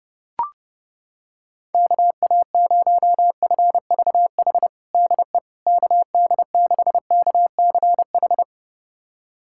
Morse code, heard as a keyed tone, about 30 wpm, 700 Hz. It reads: KA0FVH DE KD6KCH
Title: USA Callsign DE Callsign - 6 Characters - 30wpm